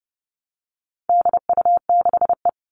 Morse code, heard as a keyed tone, about 30 wpm, 700 Hz. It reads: DU6E
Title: All Cut Numbers - 4-Digit 30wpm